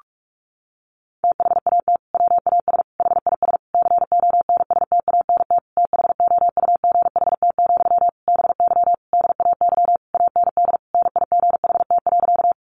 Morse code, heard as a keyed tone, about 44 wpm, 700 Hz. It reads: THAT WAS HIS CONSTANT THOUGHT, BY DAY AND NIGHT.
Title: Sentences from Top 1500 Words 45wpm